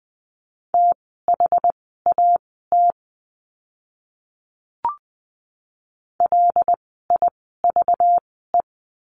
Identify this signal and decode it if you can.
Morse code, heard as a keyed tone, about 20 wpm, spaced as for 15 wpm, 700 Hz. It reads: THAT LIVE
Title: 237 Lesson 38f - Letter q - Words 20wpm